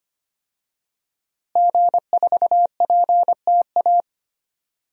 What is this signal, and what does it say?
Morse code, heard as a keyed tone, about 25 wpm, 700 Hz. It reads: G4PTA
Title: DXCC-100 Call Signs - 5 Characters - Repeated 3x - 25wpm